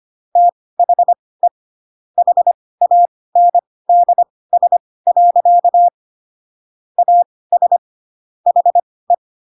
Morse code, heard as a keyed tone, about 25 wpm, spaced as for 18 wpm, 700 Hz. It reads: THE HANDS. AS HE
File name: Pinocchio ch3 12 25 ik1hge